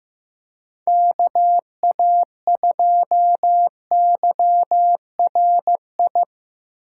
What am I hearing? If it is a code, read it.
Morse code, heard as a keyed tone, about 15 wpm, 700 Hz. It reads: KA2YRI
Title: USA Call Signs - 6 Characters - 2x Character Spacing - 15wpm